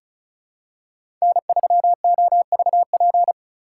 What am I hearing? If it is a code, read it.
Morse code, heard as a keyed tone, about 36 wpm, 690 Hz. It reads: N3OVP
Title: USA Call Signs - 5 Characters - 3x Character Spacing - 35wpm